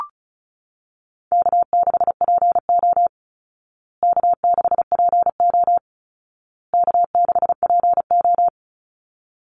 Morse code, heard as a keyed tone, about 34 wpm, 700 Hz. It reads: K6PO K6PO K6PO
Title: Call Signs Repeated 2x - 35wpm